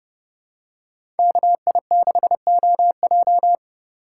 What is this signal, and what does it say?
Morse code, heard as a keyed tone, about 30 wpm, 700 Hz. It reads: KI6OJ
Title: USA Call Signs - 5 Characters - 4x Character Spacing - 30wpm